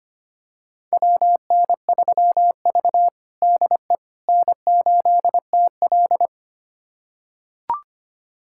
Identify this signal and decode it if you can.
Morse code, heard as a keyed tone, about 25 wpm, 700 Hz. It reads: WN3V DE N8TL
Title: USA Callsign DE Callsign - 4 Characters - 25wpm